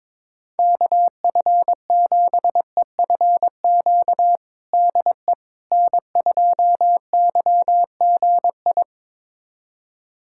Morse code, heard as a keyed tone, about 22 wpm, 700 Hz. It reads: KF7EFQ DE N2YGI
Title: USA Callsign DE Callsign 22wpm